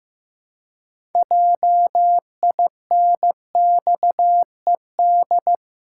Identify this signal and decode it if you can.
Morse code, heard as a keyed tone, about 15 wpm, 700 Hz. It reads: JINXED
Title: Pangram Challenge v2 - 15wpm